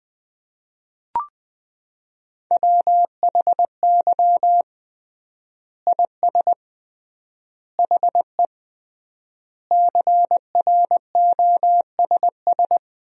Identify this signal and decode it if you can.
Morse code, heard as a keyed tone, about 20 wpm, 700 Hz. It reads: WHY IS HE CROSS
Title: 3x Word-Spacing - Sentences from Top 300 Words - 20wpm